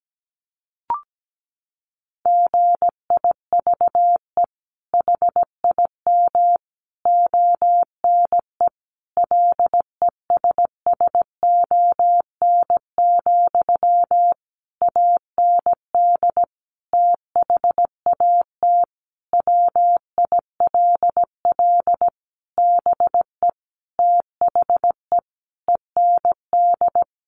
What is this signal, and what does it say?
Morse code, heard as a keyed tone, about 17 wpm, 700 Hz. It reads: GIVE HIM ONE LESSON, AND THAT WILL BE THE END
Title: Sentences from Top 1200 Words 17wpm